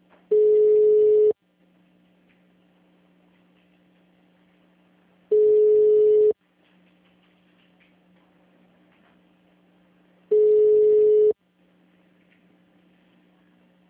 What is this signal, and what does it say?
Ringback tone (Europe, 425 Hz).